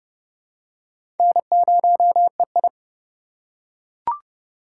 Morse code, heard as a keyed tone, about 30 wpm, 700 Hz. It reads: N0EI
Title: USA Call Signs - 4 Characters - 3x Character Spacing - 30wpm